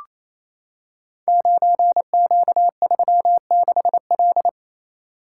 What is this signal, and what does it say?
Morse code, heard as a keyed tone, about 29 wpm, 700 Hz. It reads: 9Q36L
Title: Code Groups - 5 Letter:Numbers - 28wpm